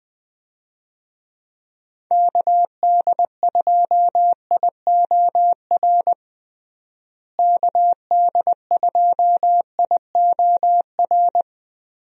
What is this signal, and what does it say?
Morse code, heard as a keyed tone, about 20 wpm, 700 Hz. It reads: KD2IOR KD2IOR
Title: USA Call Signs - 6 Characters - Repeated 2x - 20wpm